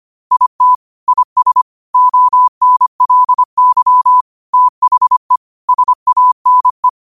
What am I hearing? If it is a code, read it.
Morse code, heard as a keyed tone, about 25 wpm, 1000 Hz. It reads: IT IS ONLY THE SANE
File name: qotdfiles / 25wpm